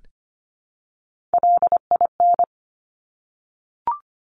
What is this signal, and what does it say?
Morse code, heard as a keyed tone, about 25 wpm, 700 Hz. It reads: LIN
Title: Top Three Letter Combinations in Words 25wpm